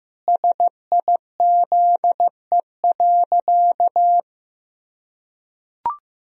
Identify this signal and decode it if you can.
Morse code, heard as a keyed tone, about 15 wpm, 700 Hz. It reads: SIZE.
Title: Sentences from Top 1000 Words - Review Part 2 - 15wpm